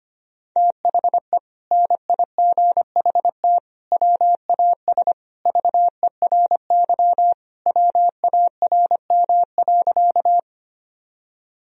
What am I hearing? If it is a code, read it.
Morse code, heard as a keyed tone, about 25 wpm, 700 Hz. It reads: THE NIGHT WAS VERY WARM.
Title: Sentences from Top 1000 Words - Review Part 2 - 25wpm